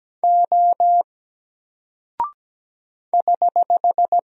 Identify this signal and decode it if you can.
Morse code, heard as a keyed tone, about 17 wpm, 700 Hz. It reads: O <HH>
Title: Single Character (Letter:Number:Punctuation:Prosign) Alt 17wpm